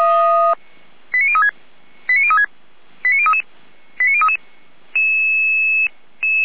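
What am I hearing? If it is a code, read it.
Five-tone selective call (ZVEI), heard as quick runs of five tones, tones of about 70 ms. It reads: 89036 89036 89033 89033
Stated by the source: ZVEI five-tone selective call received on an SDR.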